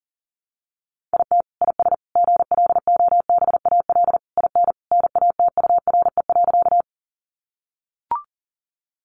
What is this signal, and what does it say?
Morse code, heard as a keyed tone, about 37 wpm, 700 Hz. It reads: IT IS GLOBAL IN NATURE.
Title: Sentences from Top 1200 Words 40wpm